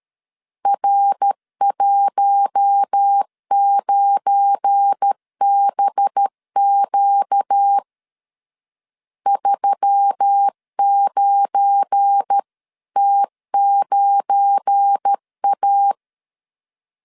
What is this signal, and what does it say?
Morse code, heard as a keyed tone, about 13 wpm, 800 Hz. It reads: R19BQ 39T9A
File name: LEZIONI / 7 GF9Z / 7 GF9 12wpm 2319